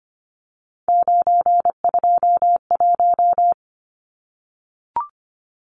Morse code, heard as a keyed tone, about 25 wpm, 700 Hz. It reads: 921
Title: Numbers - 3-Digit 25wpm